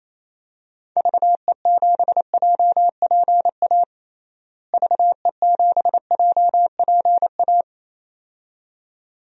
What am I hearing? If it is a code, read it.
Morse code, heard as a keyed tone, about 28 wpm, 700 Hz. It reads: VE7JPA VE7JPA
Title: DXCC-100 Call Signs - 6 Characters - Repeated 2x - 28wpm